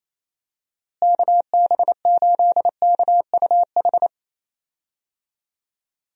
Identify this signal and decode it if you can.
Morse code, heard as a keyed tone, about 28 wpm, 700 Hz. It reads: KB8KUH
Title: USA Call Signs - 6 Characters - Repeated 2x - 28wpm